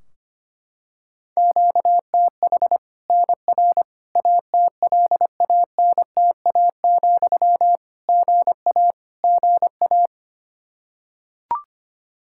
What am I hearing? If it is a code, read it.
Morse code, heard as a keyed tone, about 25 wpm, 700 Hz. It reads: QTH NR ATLANTA, GA GA
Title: K1USN SST QSO Practice 25wpm